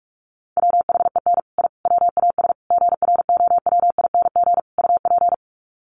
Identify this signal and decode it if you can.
Morse code, heard as a keyed tone, about 44 wpm, 700 Hz. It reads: WHEN I WAS GROWING UP
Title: Sets of 5 Words Encore 45wpm